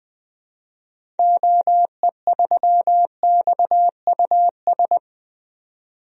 Morse code, heard as a keyed tone, about 20 wpm, 700 Hz. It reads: OE3XUS
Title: DXCC-100 Call Signs - 6 Characters - 20wpm